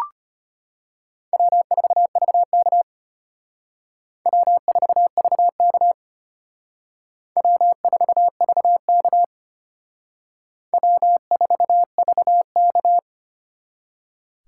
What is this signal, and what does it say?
Morse code, heard as a keyed tone, about 31 wpm, 700 Hz. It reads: W4VK W4VK W4VK W4VK
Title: Speed Racing - USA 1x2 Call Signs 25wpm